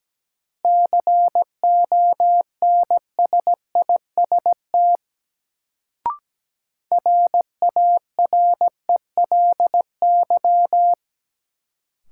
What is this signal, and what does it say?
Morse code, heard as a keyed tone, about 17 wpm, 700 Hz. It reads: CONSIST RARELY
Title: Sentences from Top 1400 Words - New Word Drill - 17wpm